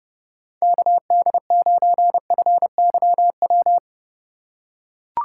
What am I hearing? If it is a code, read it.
Morse code, heard as a keyed tone, about 31 wpm, 700 Hz. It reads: KD9FYW
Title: USA Call Signs - 6 Characters - 2x Character Spacing - 30wpm